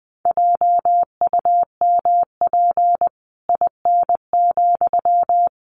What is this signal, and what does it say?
Morse code, heard as a keyed tone, about 20 wpm, 700 Hz. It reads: JUMP IN,
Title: Sentences from Top 1300 Words 20wpm